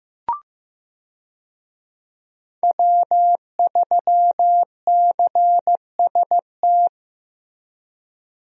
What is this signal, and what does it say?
Morse code, heard as a keyed tone, about 15 wpm, 700 Hz. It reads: W3CST
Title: USA Call Signs - 5 Characters - Repeated 2x - 15wpm